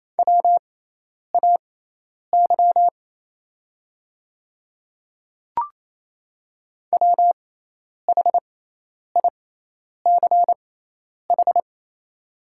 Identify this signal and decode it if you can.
Morse code, heard as a keyed tone, about 28 wpm, spaced as for 10 wpm, 700 Hz. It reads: WAY WHICH
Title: 6x Character Spacing - Top 100 Words - 28wpm